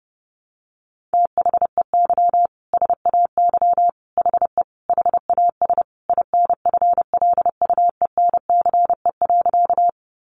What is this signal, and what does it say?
Morse code, heard as a keyed tone, about 30 wpm, 700 Hz. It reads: THEY SAY HE HAS INFLUENCE.
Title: Sentences from Top 1100 Words 30wpm